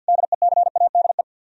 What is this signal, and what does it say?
Morse code, heard as a keyed tone, about 64 wpm, 700 Hz. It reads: DECADE